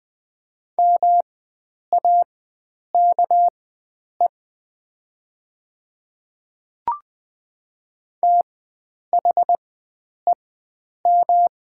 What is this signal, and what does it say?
Morse code, heard as a keyed tone, about 20 wpm, spaced as for 9 wpm, 700 Hz. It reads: MAKE THEM
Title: 4x Character Spacing - Top 100 Words - 20wpm